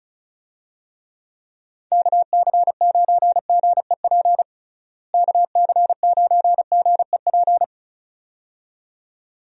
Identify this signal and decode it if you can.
Morse code, heard as a keyed tone, about 36 wpm, 700 Hz. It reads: KC9GEP KC9GEP
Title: USA Call Signs - 6 Characters - Repeated 2x - 35wpm